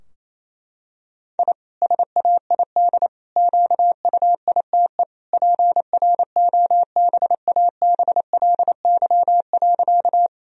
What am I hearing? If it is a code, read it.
Morse code, heard as a keyed tone, about 28 wpm, 700 Hz. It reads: I SAID QUITE PROBABLY.
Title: Sentences from Top 1000 Words - Review Part 4 - 28wpm